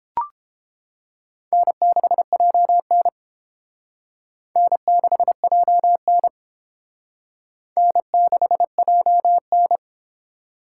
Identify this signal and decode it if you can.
Morse code, heard as a keyed tone, about 29 wpm, 700 Hz. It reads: N6JN N6JN N6JN